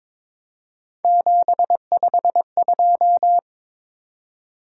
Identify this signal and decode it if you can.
Morse code, heard as a keyed tone, about 22 wpm, 700 Hz. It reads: 752
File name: practice / 22wpm / Three Letter-Number Combinations - 22wpm